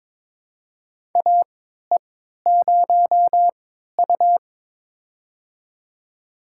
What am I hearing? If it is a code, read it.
Morse code, heard as a keyed tone, about 22 wpm, spaced as for 13 wpm, 700 Hz. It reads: AE0U
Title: USA Call Signs - 4 Characters - 3x Character Spacing - 22wpm